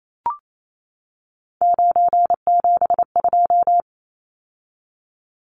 Morse code, heard as a keyed tone, about 28 wpm, 700 Hz. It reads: 972
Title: Numbers - 3-Digit 28wpm